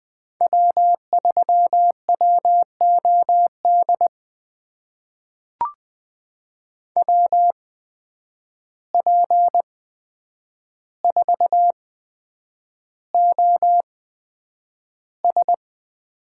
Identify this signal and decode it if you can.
Morse code, heard as a keyed tone, about 20 wpm, spaced as for 9 wpm, 700 Hz. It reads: W3WOD WP4OS
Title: USA Call Signs - 5 Characters - 8x Character Spacing - 20wpm